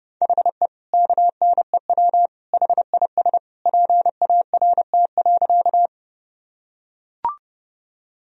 Morse code, heard as a keyed tone, about 31 wpm, 700 Hz. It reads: HE KNEW HIS PART.
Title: Sentences from Top 1000 Words - Review Part 1 - 30wpm